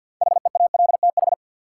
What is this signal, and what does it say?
Morse code, heard as a keyed tone, about 50 wpm, 700 Hz. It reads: HEALTH
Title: Sentences from Top 600 Words - New Word Drill - 50wpm